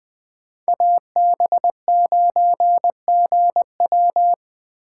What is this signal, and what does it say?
Morse code, heard as a keyed tone, about 20 wpm, 700 Hz. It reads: AB9GW
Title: Speed Racing - USA 2x2 Call Signs 17wpm